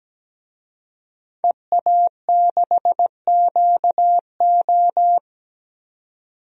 Morse code, heard as a keyed tone, about 17 wpm, 700 Hz. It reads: EA6QO
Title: DXCC-100 Call Signs - 5 Characters - 17wpm